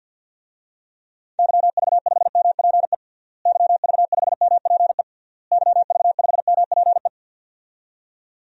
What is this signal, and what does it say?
Morse code, heard as a keyed tone, about 51 wpm, 700 Hz. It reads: YV5MPE YV5MPE YV5MPE
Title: DXCC-100 Call Signs - 6 Characters - Repeated 3x - 50wpm